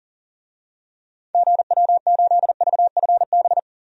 Morse code, heard as a keyed tone, about 40 wpm, 700 Hz. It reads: GW8VFB
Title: DXCC-100 Call Signs - 6 Characters - 40wpm